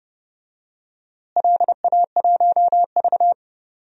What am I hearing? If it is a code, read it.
Morse code, heard as a keyed tone, about 32 wpm, 700 Hz. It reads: LA1V
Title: DXCC-100 Call Signs - 4 Characters - Repeated 2x - 30wpm